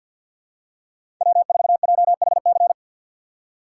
Morse code, heard as a keyed tone, about 53 wpm, 700 Hz. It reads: W4JHC
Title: USA Call Signs - 5 Characters - 3x Character Spacing - 50wpm